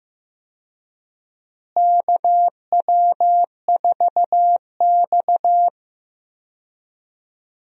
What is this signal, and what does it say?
Morse code, heard as a keyed tone, about 15 wpm, 700 Hz. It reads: KW4X
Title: USA Call Signs - 4 Characters - Repeated 2x - 15wpm